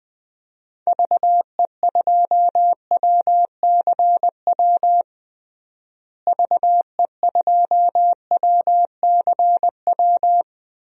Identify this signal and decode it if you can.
Morse code, heard as a keyed tone, about 21 wpm, 700 Hz. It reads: VE2WCW VE2WCW
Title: DXCC-100 Call Signs - 6 Characters - Repeated 3x - 20wpm